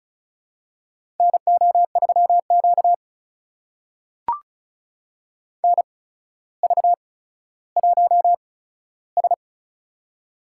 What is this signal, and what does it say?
Morse code, heard as a keyed tone, about 36 wpm, spaced as for 15 wpm, 700 Hz. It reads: NO3Q NV1S
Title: USA Call Signs - 4 Characters - 8x Character Spacing - 35wpm